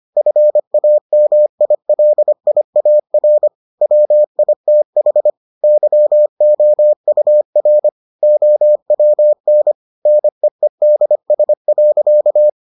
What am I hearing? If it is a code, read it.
Morse code, heard as a keyed tone, about 25 wpm, 600 Hz. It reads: FAMILIAR WITH YOUR OWN NEEDS.